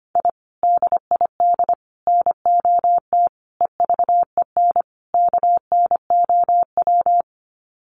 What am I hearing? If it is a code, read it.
Morse code, heard as a keyed tone, about 25 wpm, 700 Hz. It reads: I DID NOT EVEN KNOW
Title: Sets of 5 Words Encore 25wpm